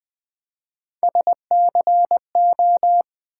Morse code, heard as a keyed tone, about 20 wpm, 700 Hz. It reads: SCO
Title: Build A Word - 1 Letter At A Time - 5-Letter Words - 20wpm